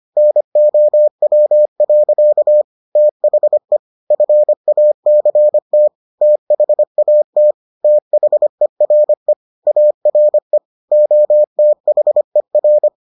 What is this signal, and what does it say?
Morse code, heard as a keyed tone, about 25 wpm, 600 Hz. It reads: NOW. THE FACT THAT THERE ARE OTHER